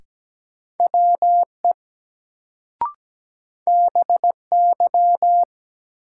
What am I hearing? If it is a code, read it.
Morse code, heard as a keyed tone, about 17 wpm, 700 Hz. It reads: WE BY
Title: Sentences from Top 100 Words - New Word Drill - 17wpm